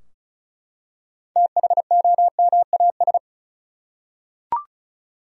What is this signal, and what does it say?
Morse code, heard as a keyed tone, about 34 wpm, 700 Hz. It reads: THOMAS
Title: USA Common Names - 35wpm